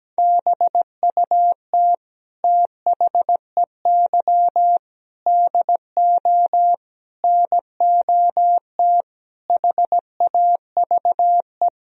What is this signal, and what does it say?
Morse code, heard as a keyed tone, about 17 wpm, 700 Hz. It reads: BUT THEY DO NOT HAVE
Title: Sentences from Top 1000 Words - Review Part 4 - 17wpm